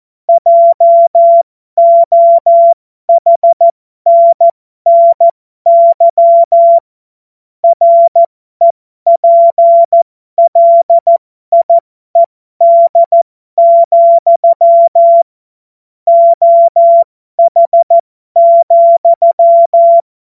Morse code, heard as a keyed tone, about 14 wpm, spaced as for 12 wpm, 680 Hz. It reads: JOHNNY REPLIED, OH,
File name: Morsekurs / Witze